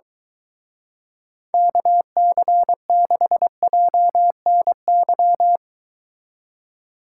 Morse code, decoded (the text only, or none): KC6JNY